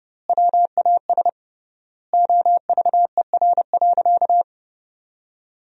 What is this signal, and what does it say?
Morse code, heard as a keyed tone, about 30 wpm, 700 Hz. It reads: WAS OVER.
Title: 3x Word-Spacing - Sentences from Top 200 Words - 30wpm